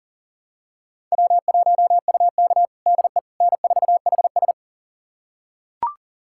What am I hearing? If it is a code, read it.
Morse code, heard as a keyed tone, about 37 wpm, 700 Hz. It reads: W1UK DE N4HS
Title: USA Callsign DE Callsign - 4 Characters - 40wpm